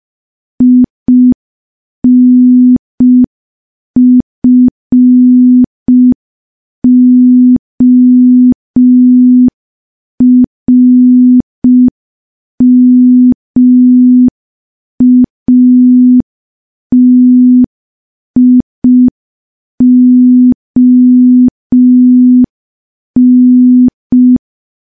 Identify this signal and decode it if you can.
Morse code, heard as a keyed tone, about 5 wpm, 250 Hz. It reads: INFORMATION
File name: morse / morse-training-05WPM